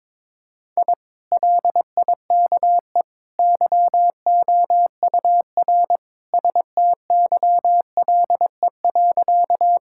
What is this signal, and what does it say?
Morse code, heard as a keyed tone, about 23 wpm, 700 Hz. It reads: I LIKE YOUR STYLE.
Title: Sentences from Top 1000 Words - Review Part 5 - 22wpm